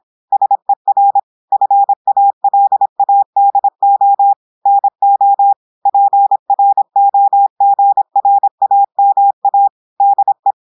Morse code, heard as a keyed tone, about 26 wpm, 800 Hz. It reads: SER FALADO NO PROGRAMA DE
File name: portugais texte en clair 2 vitesse 26 mots minutes